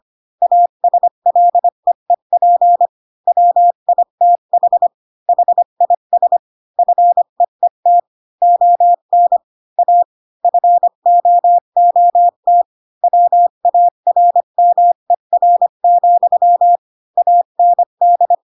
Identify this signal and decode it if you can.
Morse code, heard as a keyed tone, about 25 wpm, 700 Hz. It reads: ASLEEP WITH HIS FEET ON A FOOT WARMER, AND
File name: Pinocchio / Pinocchio ch6 20 25 ik1hge